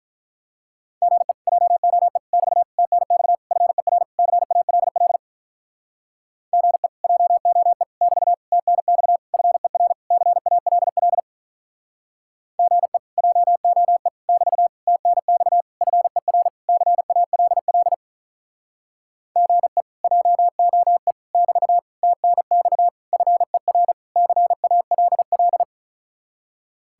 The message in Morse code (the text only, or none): GE JOE <BT> TNX FER CALL GE JOE <BT> TNX FER CALL GE JOE <BT> TNX FER CALL GE JOE <BT> TNX FER CALL